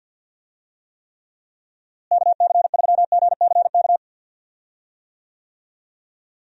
Morse code, heard as a keyed tone, about 50 wpm, 700 Hz. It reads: KC3GCK
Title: USA Call Signs - 6 Characters - Repeated 3x - 50wpm